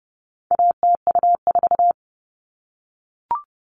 Morse code, heard as a keyed tone, about 30 wpm, 700 Hz. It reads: ATU4
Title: All Cut Numbers - 4-Digit 30wpm